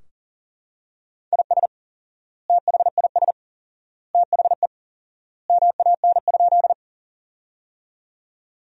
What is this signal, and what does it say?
Morse code, heard as a keyed tone, about 37 wpm, 700 Hz. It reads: IS THIS THE MAN?
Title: 4x Word-Spacing - Sentences from Top 200 Words - 40wpm